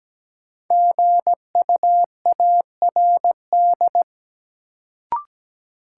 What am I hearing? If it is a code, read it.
Morse code, heard as a keyed tone, about 17 wpm, 700 Hz. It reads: GUARD
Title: Sentences from Top 700 Words - New Word Drill - 17wpm